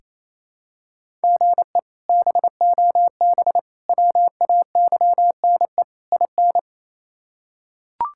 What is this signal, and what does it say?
Morse code, heard as a keyed tone, about 28 wpm, 700 Hz. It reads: GE BOB WAYNE IN
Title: K1USN SST QSO Practice - 28wpm